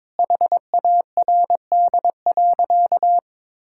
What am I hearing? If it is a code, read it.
Morse code, heard as a keyed tone, about 21 wpm, 700 Hz. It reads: HARD.
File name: Sentences with 7-Word Phrases - 22wpm